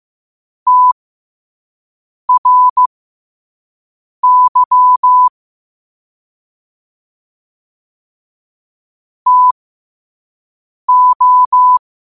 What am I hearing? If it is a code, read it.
Morse code, heard as a keyed tone, about 15 wpm, spaced as for 5 wpm, 1000 Hz. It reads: TRY TO
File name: qotdfiles / 5wpm